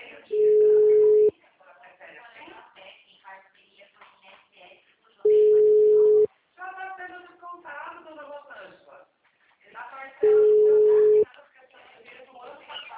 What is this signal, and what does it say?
Ringback tone (Europe, 425 Hz), with speech.